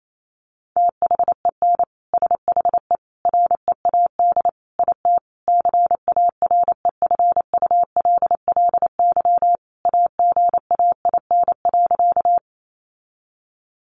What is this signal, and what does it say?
Morse code, heard as a keyed tone, about 28 wpm, 700 Hz. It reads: THEN SHE READ IT CAREFULLY AGAIN.
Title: Sentences from Top 1300 Words 28wpm